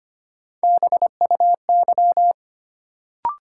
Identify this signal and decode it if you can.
Morse code, heard as a keyed tone, about 25 wpm, 700 Hz. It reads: BUY
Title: Sentences from Top 700 Words - New Word Drill - 25wpm